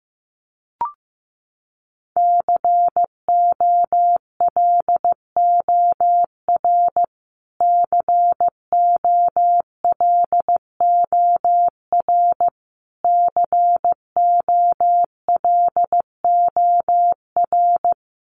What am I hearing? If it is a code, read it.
Morse code, heard as a keyed tone, about 15 wpm, 700 Hz. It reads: COLOR COLOR COLOR
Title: Top 300 Words Repeated 15wpm